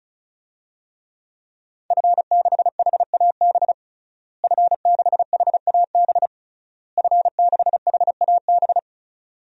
Morse code, heard as a keyed tone, about 36 wpm, 700 Hz. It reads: F6HAB F6HAB F6HAB
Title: DXCC-100 Call Signs - 5 Characters - Repeated 3x - 35wpm